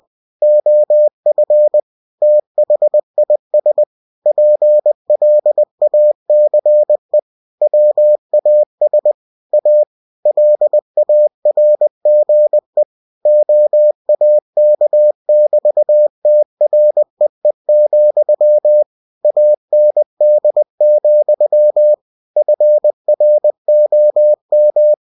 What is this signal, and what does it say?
Morse code, heard as a keyed tone, about 20 wpm, 600 Hz. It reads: OF THIS PLACE WAS A LARGE OAK<BT>TREE, AND, FROM
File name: cw-books / winnie-the-pooh / winnie-20wpm-page-0002